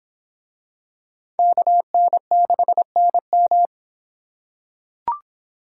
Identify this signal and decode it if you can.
Morse code, heard as a keyed tone, about 26 wpm, 700 Hz. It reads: KN6NM